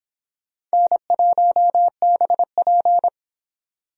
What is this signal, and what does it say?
Morse code, heard as a keyed tone, about 26 wpm, 700 Hz. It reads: N1BP